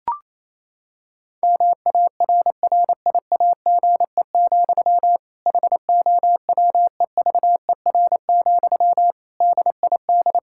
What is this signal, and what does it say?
Morse code, heard as a keyed tone, about 28 wpm, 700 Hz. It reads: MARRIAGE, HOWEVER, DID
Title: Sentences from Top 1000 Words - Review Part 5 - 28wpm